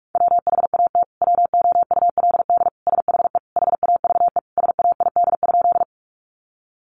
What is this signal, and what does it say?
Morse code, heard as a keyed tone, about 44 wpm, 700 Hz. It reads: WHAT WOULD SHE HAVE SAID?
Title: Sentences from Top 100 Words 45wpm